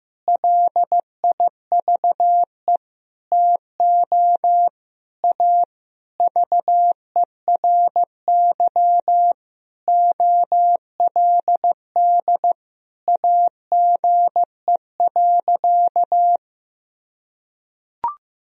Morse code, heard as a keyed tone, about 15 wpm, 700 Hz. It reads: LIVE TO A VERY OLD AGE.
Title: Sentences from Top 1000 Words - Review Part 3 - 15wpm